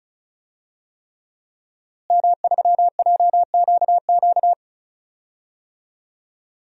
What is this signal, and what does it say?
Morse code, heard as a keyed tone, about 34 wpm, 700 Hz. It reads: M3JQQ
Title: DXCC-100 Call Signs - 5 Characters - Repeated 3x - 35wpm